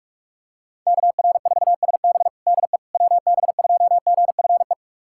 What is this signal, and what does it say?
Morse code, heard as a keyed tone, about 43 wpm, 700 Hz. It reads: KR4SB DE WB2GFE
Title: USA Callsign DE Callsign 45wpm